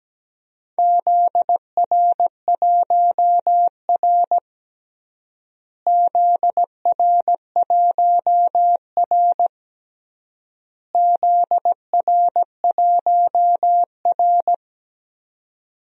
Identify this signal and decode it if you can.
Morse code, heard as a keyed tone, about 17 wpm, 700 Hz. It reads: ZR1R ZR1R ZR1R
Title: DXCC-100 Call Signs - 4 Characters - Repeated 3x - 17wpm